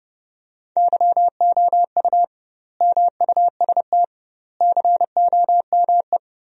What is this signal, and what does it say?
Morse code, heard as a keyed tone, about 32 wpm, 700 Hz. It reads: YOU MUST COME
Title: 2x Word-Spacing - Sentences from Top 200 Words - 30wpm